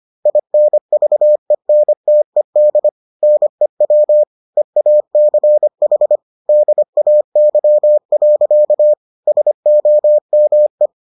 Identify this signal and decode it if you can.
Morse code, heard as a keyed tone, about 25 wpm, 600 Hz. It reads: INVENTED NEW EACH DAY. SOME